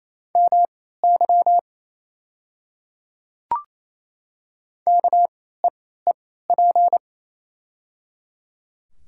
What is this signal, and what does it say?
Morse code, heard as a keyed tone, about 28 wpm, spaced as for 16 wpm, 700 Hz. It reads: MY KEEP
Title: 3x Character Spacing - Top 500 Words - 28wpm